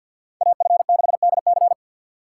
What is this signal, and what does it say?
Morse code, heard as a keyed tone, about 64 wpm, 700 Hz. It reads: AF6DC